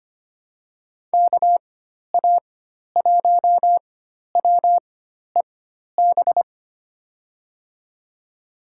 Morse code, heard as a keyed tone, about 25 wpm, spaced as for 12 wpm, 700 Hz. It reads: KA1WEB
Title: USA Call Signs - 6 Characters - 4x Character Spacing - 25wpm